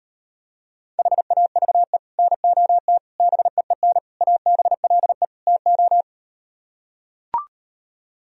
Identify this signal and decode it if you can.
Morse code, heard as a keyed tone, about 38 wpm, 700 Hz. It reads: HAVE NOT BEEN ABLE TO